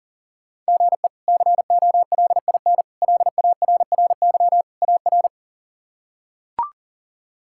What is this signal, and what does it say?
Morse code, heard as a keyed tone, about 40 wpm, 700 Hz. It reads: GE COLIN LARRY AR
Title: K1USN SST QSO Practice - 40wpm